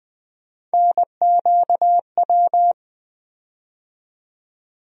Morse code, heard as a keyed tone, about 20 wpm, 700 Hz. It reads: NQW
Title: USA Call Sign Postfixes - 3 Letters 20wpm